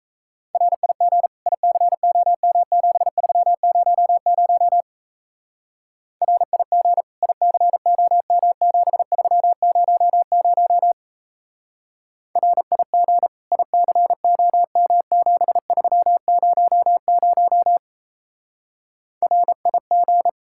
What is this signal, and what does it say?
Morse code, heard as a keyed tone, about 35 wpm, 700 Hz. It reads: RIG ICOM7300 RIG ICOM7300 RIG ICOM7300 RIG
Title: Speed-Racing Common QSO Phrases Encore - 28wpm